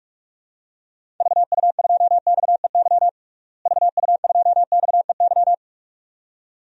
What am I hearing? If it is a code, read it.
Morse code, heard as a keyed tone, about 46 wpm, 700 Hz. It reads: VU2XEY VU2XEY
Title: DXCC-100 Call Signs - 6 Characters - Repeated 2x - 45wpm